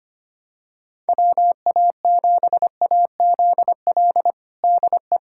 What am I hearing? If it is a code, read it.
Morse code, heard as a keyed tone, about 25 wpm, 700 Hz. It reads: WA7AZL DE
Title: USA Callsign DE Callsign - 6 Characters - 25wpm